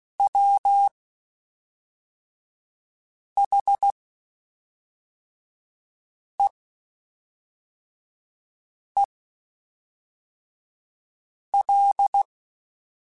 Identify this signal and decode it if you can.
Morse code, heard as a keyed tone, about 16 wpm, spaced as for 3 wpm, 800 Hz. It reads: WHEEL
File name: Spaced / Chuckles Spaced-mp3s / Squeaky sp @5